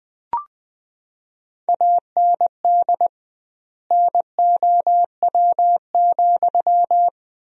Morse code, heard as a keyed tone, about 21 wpm, 700 Hz. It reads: AND NOW,
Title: 2x Word-Spacing - Sentences from Top 300 Words - 20wpm